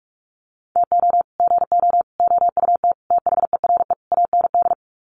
Morse code, heard as a keyed tone, about 44 wpm, 690 Hz. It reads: TO GO OUT THERE AND
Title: Sets of 5 Words Encore2 45wpm